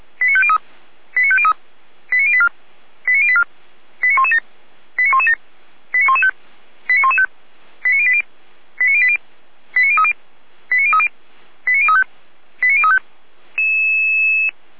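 Five-tone selective call (ZVEI), heard as quick runs of five tones, tones of about 70 ms. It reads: 89503 89503 89084 89084 89117 89117 89115 89115 89080 89080 89030 89030 89035 89035
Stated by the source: ZVEI five-tone selective call received by SDR.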